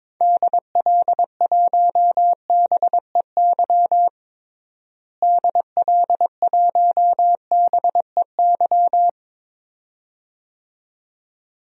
Morse code, heard as a keyed tone, about 22 wpm, 700 Hz. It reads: DL1BEY DL1BEY
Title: DXCC-100 Call Signs - 6 Characters - Repeated 2x - 22wpm